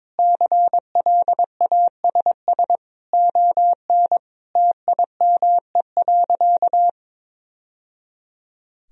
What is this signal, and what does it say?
Morse code, heard as a keyed tone, about 22 wpm, 700 Hz. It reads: CLASS ON TIME.
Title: Sentences from Top 1000 Words - Review Part 2 - 22wpm